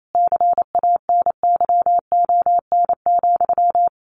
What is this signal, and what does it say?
Morse code, heard as a keyed tone, about 28 wpm, 700 Hz. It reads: CANYON,
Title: A Sound Of Thunder 28wpm